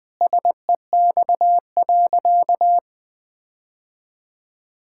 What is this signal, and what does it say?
Morse code, heard as a keyed tone, about 20 wpm, 700 Hz. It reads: SEX.
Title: Sentences from Top 1600 Words 20wpm